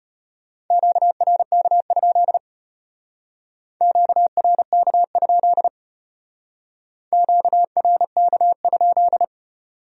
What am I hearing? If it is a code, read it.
Morse code, heard as a keyed tone, about 30 wpm, 700 Hz. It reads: QRK? QRK? QRK?
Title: Speed-Racing Q-Codes - 25wpm